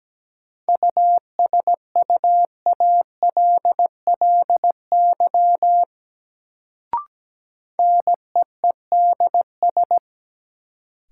Morse code, heard as a keyed tone, about 17 wpm, 700 Hz. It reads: USUALLY NEEDS
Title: Sentences from Top 1000 Words - Review Part 3 - New Word Drill - 17wpm